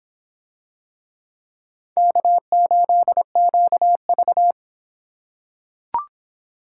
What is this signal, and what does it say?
Morse code, heard as a keyed tone, about 26 wpm, 700 Hz. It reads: K8QV